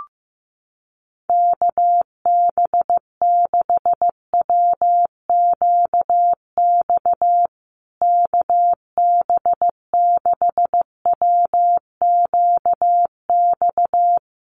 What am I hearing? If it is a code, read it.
Morse code, heard as a keyed tone, about 15 wpm, 700 Hz. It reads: KB6WQX KB6WQX
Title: USA Call Signs Repeated 3x 15wpm